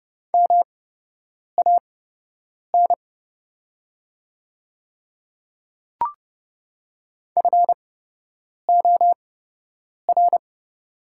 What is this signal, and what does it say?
Morse code, heard as a keyed tone, about 30 wpm, spaced as for 8 wpm, 700 Hz. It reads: MAN FOR
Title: 8x Character Spacing - Top 200 Words - 30wpm